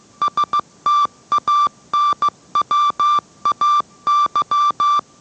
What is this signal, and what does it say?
Morse code, heard as a keyed tone, about 15 wpm, 1170 Hz. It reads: STANWAY